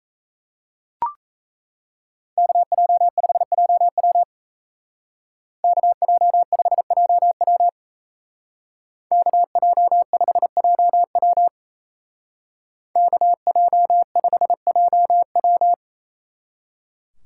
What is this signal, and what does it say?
Morse code, heard as a keyed tone, about 36 wpm, 700 Hz. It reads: KJ5JW KJ5JW KJ5JW KJ5JW